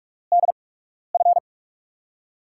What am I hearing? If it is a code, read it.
Morse code, heard as a keyed tone, about 44 wpm, spaced as for 12 wpm, 700 Hz. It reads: DF
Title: USA Call Signs - 6 Characters - 8x Character Spacing - 45wpm